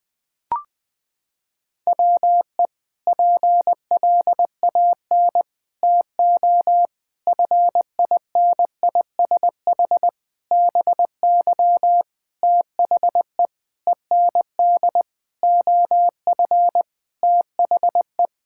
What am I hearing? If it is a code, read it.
Morse code, heard as a keyed tone, about 20 wpm, 700 Hz. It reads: WE PLAN TO FINISH BY THE END OF THE
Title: Sentences with 5-Word Phrases - 20wpm